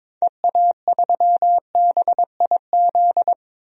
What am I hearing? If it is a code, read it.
Morse code, heard as a keyed tone, about 21 wpm, 700 Hz. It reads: EA3BIZ
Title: DXCC-100 Call Signs - 6 Characters - 22wpm